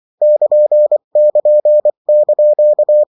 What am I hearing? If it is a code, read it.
Morse code, heard as a keyed tone, about 24 wpm, 600 Hz. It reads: <KN><KN>)
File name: morse mp3 / lecon16-vitesse 22